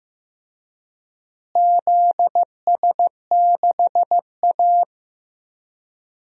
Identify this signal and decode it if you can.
Morse code, heard as a keyed tone, about 15 wpm, 700 Hz. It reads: ZS6A
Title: DXCC-100 Call Signs - 4 Characters - Repeated 3x - 15wpm